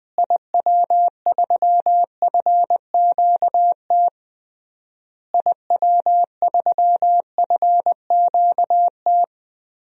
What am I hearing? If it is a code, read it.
Morse code, heard as a keyed tone, about 21 wpm, 700 Hz. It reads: IW3FQT IW3FQT
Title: DXCC-100 Call Signs - 6 Characters - Repeated 3x - 20wpm